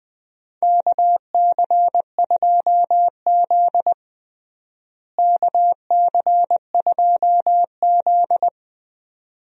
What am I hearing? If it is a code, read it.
Morse code, heard as a keyed tone, about 20 wpm, 700 Hz. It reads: KC2Z KC2Z
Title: USA Call Signs - 4 Characters - Repeated 2x - 20wpm